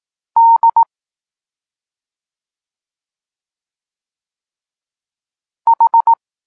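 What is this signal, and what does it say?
Morse code, heard as a keyed tone, about 18 wpm, 920 Hz. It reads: D H
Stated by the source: Morse code lesson for learning the letters K and X, from the SP9MOA Niepołomicki Klub Krótkofalowców course.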